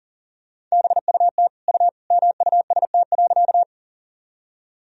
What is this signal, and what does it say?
Morse code, heard as a keyed tone, about 40 wpm, 700 Hz. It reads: BUT U MUST.
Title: Sentences from Top 200 Words and Abbreviations 40wpm